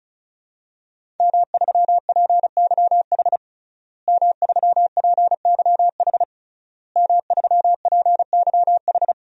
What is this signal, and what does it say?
Morse code, heard as a keyed tone, about 36 wpm, 700 Hz. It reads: M3PYH M3PYH M3PYH
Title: DXCC-100 Call Signs - 5 Characters - Repeated 3x - 35wpm